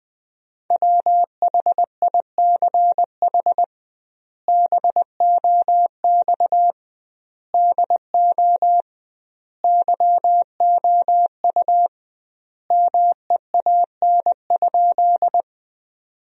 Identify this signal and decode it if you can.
Morse code, heard as a keyed tone, about 21 wpm, 700 Hz. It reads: WHICH BOX DO YOU MEAN?
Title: 2x Word-Spacing - Sentences from Top 500 Words - 20wpm